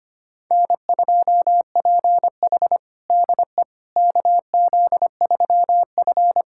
Morse code, heard as a keyed tone, about 25 wpm, 700 Hz. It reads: N2PH DE KZ3F
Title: USA Callsign DE Callsign - 4 Characters - 25wpm